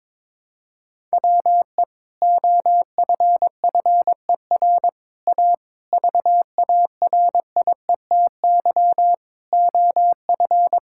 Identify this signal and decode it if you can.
Morse code, heard as a keyed tone, about 21 wpm, 700 Hz. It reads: WE OFFER A VARIETY OF
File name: practice / 22wpm / Sentences with 3-Word Phrases - 22wpm